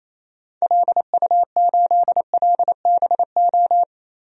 Morse code, heard as a keyed tone, about 28 wpm, 700 Hz. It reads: LU8LBO
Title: DXCC-100 Call Signs - 6 Characters - Repeated 2x - 28wpm